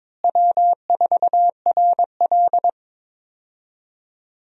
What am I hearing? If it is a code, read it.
Morse code, heard as a keyed tone, about 22 wpm, 700 Hz. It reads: W4RL
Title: USA Call Signs - 4 Characters - Repeated 2x - 22wpm